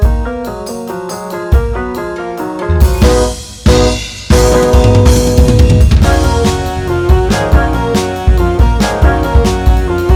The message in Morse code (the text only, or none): none